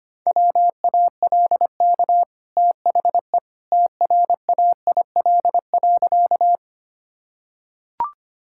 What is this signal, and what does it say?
Morse code, heard as a keyed tone, about 25 wpm, 690 Hz. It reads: WALK THE TRAIL.
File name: Easy Sentences with Hints - 25wpm